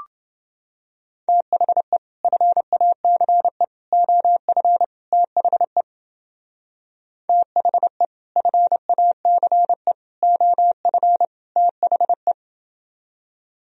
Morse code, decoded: THE FACE OF THE THE FACE OF THE

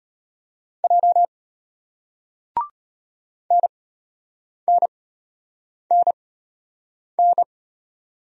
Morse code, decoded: J NNNN

